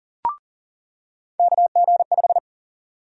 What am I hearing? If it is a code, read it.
Morse code, heard as a keyed tone, about 40 wpm, 700 Hz. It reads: KG5
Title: USA Call Sign Prefixes - 40wpm